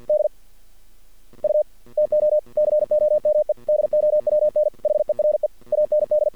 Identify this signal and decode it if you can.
Morse code, heard as a keyed tone, about 48 wpm, 600 Hz. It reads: D U TO COME DOWN HERE AND
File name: archived / CRSnet